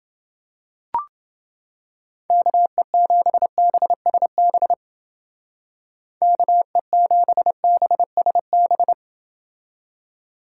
Morse code, heard as a keyed tone, about 29 wpm, 700 Hz. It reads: KE7BSB KE7BSB